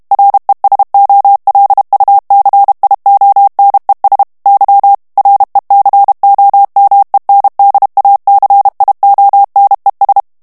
Morse code, heard as a keyed tone, about 32 wpm, 800 Hz. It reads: RESOLUCIONES Y RECOMENDACIONES
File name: morse mp3 / ESPAGNOL TEXTE EN CLAIR 2 VITESSE 30 MOTS MINUTES